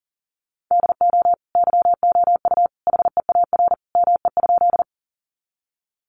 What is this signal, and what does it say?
Morse code, heard as a keyed tone, about 37 wpm, 700 Hz. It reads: DO YOU HEAR ME?
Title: Sentences from Top 300 Words 40wpm